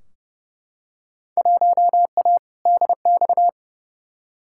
Morse code, heard as a keyed tone, about 30 wpm, 700 Hz. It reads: 1A DX
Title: Field Day Exchanges Encore - 30wpm